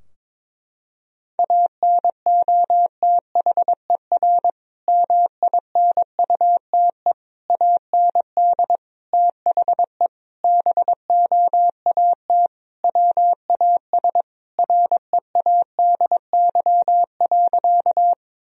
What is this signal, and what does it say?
Morse code, heard as a keyed tone, about 22 wpm, 700 Hz. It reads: ANOTHER MINUTE AND THE BOAT WAS READY.
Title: Sentences from Top 1000 Words - Review Part 3 - 22wpm